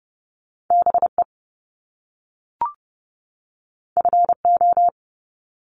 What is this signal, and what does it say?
Morse code, heard as a keyed tone, about 30 wpm, 700 Hz. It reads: BE FO
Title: Top Two Letter Combinations in Words 30wpm